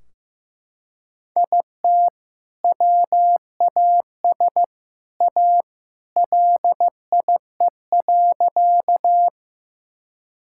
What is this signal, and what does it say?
Morse code, heard as a keyed tone, about 15 wpm, 700 Hz. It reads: IT WAS A LIE.
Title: Sentences from Top 1600 Words 15wpm